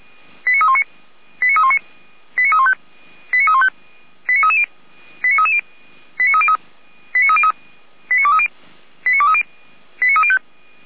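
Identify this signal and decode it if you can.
Five-tone selective call (ZVEI), heard as quick runs of five tones, tones of about 70 ms. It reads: 89319 89319 89315 89315 89330 89330 89303 89303 89230 89230 89305
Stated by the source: ZVEI five-tone selective call 89125, received by SDR.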